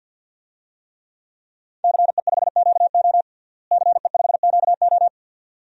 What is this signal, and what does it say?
Morse code, heard as a keyed tone, about 47 wpm, 700 Hz. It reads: CE5QO CE5QO
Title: DXCC-100 Call Signs - 5 Characters - Repeated 2x - 50wpm